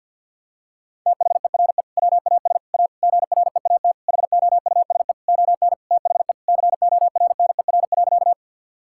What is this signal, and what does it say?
Morse code, heard as a keyed tone, about 50 wpm, 700 Hz. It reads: THERE WAS A GREAT HOUSE ON THE CORNER.
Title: Sentences from Top 1600 Words 50wpm